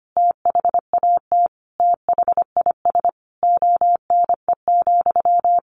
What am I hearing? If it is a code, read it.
Morse code, heard as a keyed tone, about 25 wpm, 700 Hz. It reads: THAT THIS ONE,
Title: Sentences from Top 200 Words 25wpm